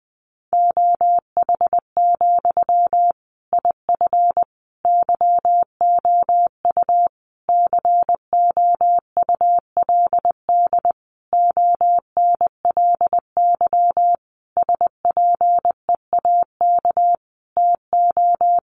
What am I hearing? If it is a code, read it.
Morse code, heard as a keyed tone, about 20 wpm, 700 Hz. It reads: OH, IF YOU COULD ONLY SPEAK TO
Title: Sentences from Top 1100 Words 20wpm